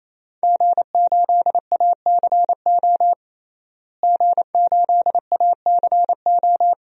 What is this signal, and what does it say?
Morse code, heard as a keyed tone, about 27 wpm, 700 Hz. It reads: G8ACO G8ACO
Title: DXCC-100 Call Signs - 5 Characters - Repeated 3x - 28wpm